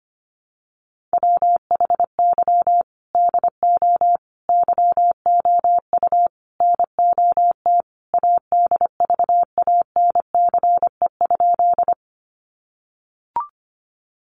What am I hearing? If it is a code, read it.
Morse code, heard as a keyed tone, about 25 wpm, 700 Hz. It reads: WHY DO YOU NOT ADVANCE?
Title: Sentences from Top 1400 Words 25wpm